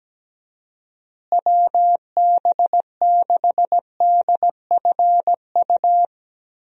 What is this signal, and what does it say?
Morse code, heard as a keyed tone, about 17 wpm, 700 Hz. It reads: WB6DFU
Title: USA Call Signs - 6 Characters - Repeated 2x - 17wpm